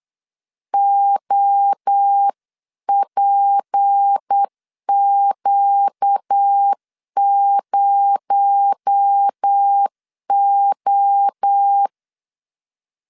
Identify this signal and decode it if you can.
Morse code, heard as a keyed tone, about 8 wpm, 800 Hz. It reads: OPQ0O
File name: CW / DET / 1 6 1500C S B 8wpm 5142